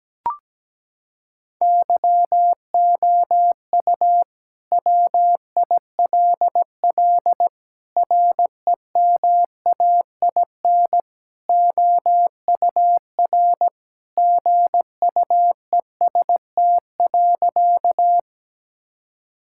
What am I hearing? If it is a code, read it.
Morse code, heard as a keyed tone, about 17 wpm, 700 Hz. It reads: YOU WILL REMAIN OUR GUEST.
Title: Sentences from Top 1600 Words 17wpm